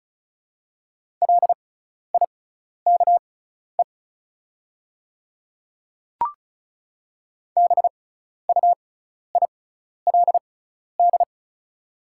Morse code, heard as a keyed tone, about 36 wpm, spaced as for 12 wpm, 690 Hz. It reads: LIKE BUILD
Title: 6x Character Spacing - Top 300 Words - 35wpm